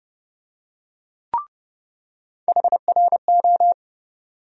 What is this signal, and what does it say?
Morse code, heard as a keyed tone, about 29 wpm, 700 Hz. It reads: HRO
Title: USA Call Sign Postfixes - 3 Letters 30wpm